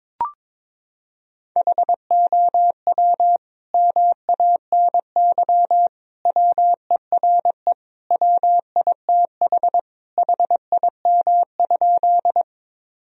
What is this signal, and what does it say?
Morse code, heard as a keyed tone, about 22 wpm, 700 Hz. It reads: HOW MANY WERE WITH HIM?
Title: Sentences from Top 1000 Words - Review Part 1 - 22wpm